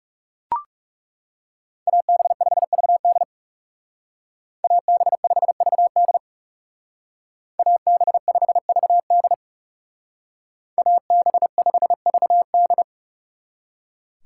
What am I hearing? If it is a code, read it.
Morse code, heard as a keyed tone, about 35 wpm, 700 Hz. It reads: AB5VD AB5VD AB5VD AB5VD
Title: Speed Racing - USA 2x2 Call Signs 30wpm